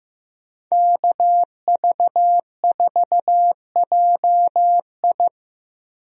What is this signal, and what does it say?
Morse code, heard as a keyed tone, about 15 wpm, 700 Hz. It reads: KV4JI
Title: USA Call Signs - 5 Characters - Repeated 3x - 15wpm